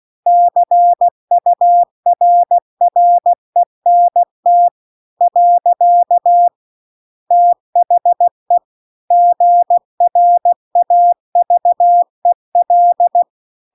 Morse code, heard as a keyed tone, about 16 wpm, 700 Hz. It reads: CURRENT . THE GRAVEL
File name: CW-The-Big-Twoheaded-River-2-16WPM